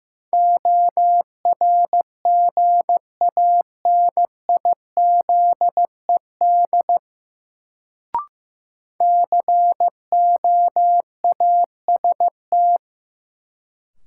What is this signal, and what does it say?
Morse code, heard as a keyed tone, about 15 wpm, 700 Hz. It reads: ORGANIZED COAST
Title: Sentences from Top 1300 Words - New Word Drill Alt - 15wpm